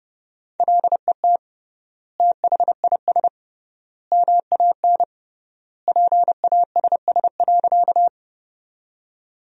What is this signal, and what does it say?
Morse code, heard as a keyed tone, about 30 wpm, 700 Hz. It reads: LET THIS MAN PASS.
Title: 3x Word-Spacing - Sentences from Top 400 Words - 30wpm